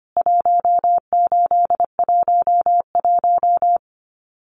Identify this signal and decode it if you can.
Morse code, heard as a keyed tone, about 24 wpm, 700 Hz. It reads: 1811
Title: CWT Exchange 25wpm